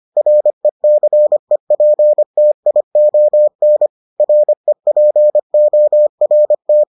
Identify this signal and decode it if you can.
Morse code, heard as a keyed tone, about 25 wpm, 600 Hz. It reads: RECEPTION REPORT